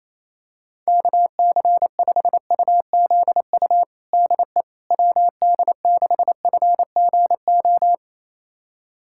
Morse code, heard as a keyed tone, about 27 wpm, 700 Hz. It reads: KC5UZU DE WD6FGO
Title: USA Callsign DE Callsign 28wpm